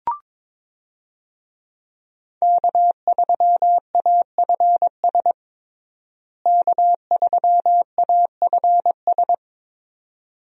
Morse code, heard as a keyed tone, about 22 wpm, 700 Hz. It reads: K3AFS K3AFS
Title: USA Call Signs - 5 Characters - Repeated 2x - 22wpm